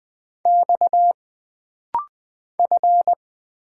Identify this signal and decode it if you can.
Morse code, heard as a keyed tone, about 20 wpm, 700 Hz. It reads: X F
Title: Single Letters - Rapid-Fire 20wpm